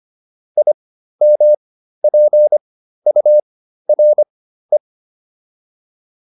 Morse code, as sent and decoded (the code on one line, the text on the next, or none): .. -- .--. ..- .-. .
IMPURE